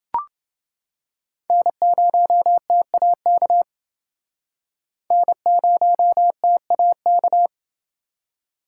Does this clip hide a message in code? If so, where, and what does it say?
Yes: Morse code from 1.5 to 7.5 s, heard as a keyed tone, about 31 wpm, 700 Hz. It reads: N0TAK N0TAK